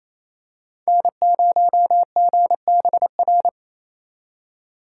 Morse code, heard as a keyed tone, about 28 wpm, 700 Hz. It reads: N0GBR
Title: USA Call Signs - 2x Character Spacing - 28wpm